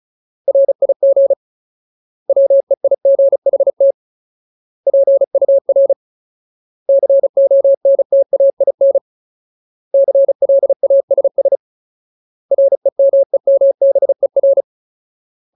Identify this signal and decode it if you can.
Morse code, heard as a keyed tone, about 34 wpm, 550 Hz. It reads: RIG WEIGHT PUR CONTAIN CLASS REMEMBER
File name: Ultimate Word List / 35WPM Ultimate